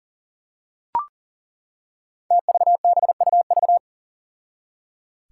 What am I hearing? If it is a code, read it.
Morse code, heard as a keyed tone, about 39 wpm, 700 Hz. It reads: TVBUV
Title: Five Challenging Letter-Number Combinations 40wpm